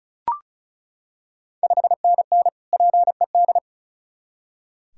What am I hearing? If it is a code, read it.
Morse code, heard as a keyed tone, about 36 wpm, 690 Hz. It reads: 5NN PED
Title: Ontario QSO Party Exchange 35wpm